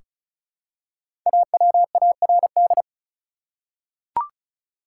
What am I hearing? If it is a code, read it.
Morse code, heard as a keyed tone, about 36 wpm, 700 Hz. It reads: AWARD
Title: Five Letter Words - 35wpm